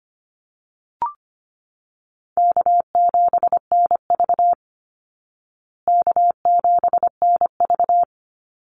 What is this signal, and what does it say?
Morse code, heard as a keyed tone, about 25 wpm, 700 Hz. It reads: K7NV K7NV
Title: Easy Call Signs Repeated 3x - 25wpm